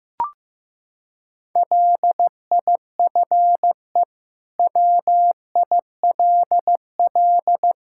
Morse code, heard as a keyed tone, about 15 wpm, 700 Hz. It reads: LIFE WILL
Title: Sentences from Top 1000 Words - Review Part 5 - 15wpm